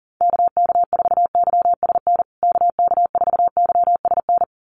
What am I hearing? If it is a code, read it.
Morse code, heard as a keyed tone, about 40 wpm, 700 Hz. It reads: KK4YSN KK4YSN
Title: USA Call Signs Repeated 2x 40wpm